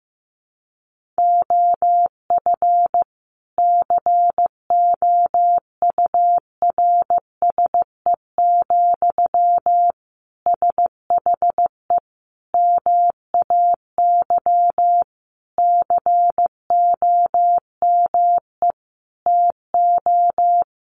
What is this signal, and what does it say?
Morse code, heard as a keyed tone, about 14 wpm, 700 Hz. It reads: OF COURSE, SHE MAY COME TO
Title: Sentences from Top 1400 Words 15wpm